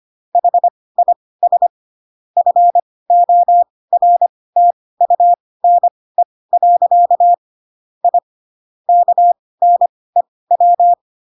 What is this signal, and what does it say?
Morse code, heard as a keyed tone, about 25 wpm, spaced as for 18 wpm, 700 Hz. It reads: HIS FORTUNE. I KNEW
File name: Pinocchio ch3 12 25 ik1hge